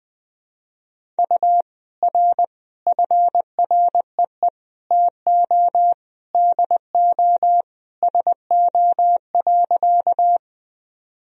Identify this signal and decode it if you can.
Morse code, heard as a keyed tone, about 20 wpm, 700 Hz. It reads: U R FREE TO DO SO.
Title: Sentences from Top 500 Words and Abbreviations 20wpm